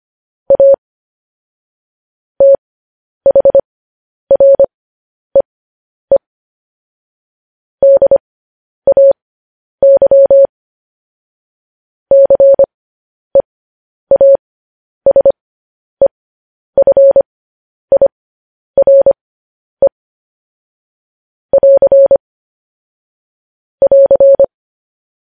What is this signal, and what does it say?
Morse code, heard as a keyed tone, about 25 wpm, spaced as for 10 wpm, 550 Hz. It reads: A THREE DAY CEASEFIRE <AR> <AR>
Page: news headlines in morse code at 10 wpm